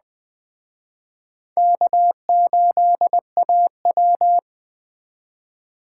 Morse code, heard as a keyed tone, about 21 wpm, 700 Hz. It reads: K8AW